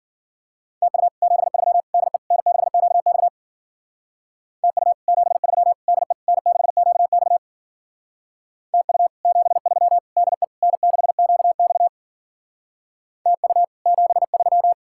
Morse code, heard as a keyed tone, about 48 wpm, 700 Hz. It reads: TU 73 DE N6QX TU 73 DE N6QX TU 73 DE N6QX TU 73
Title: Speed-Racing Common QSO Phrases Encore - 40wpm